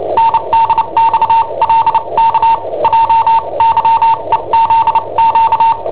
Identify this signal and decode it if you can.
Morse code, heard as a keyed tone, about 29 wpm, spaced as for 25 wpm, 910 Hz. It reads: NDXLK JYEZQ
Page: Cassetta esercizi preparata da I7OHP per IW5ELP